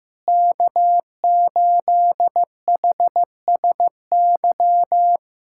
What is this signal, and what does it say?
Morse code, heard as a keyed tone, about 16 wpm, 700 Hz. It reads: K8HSY
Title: K1USN SST QSO Practice - 15wpm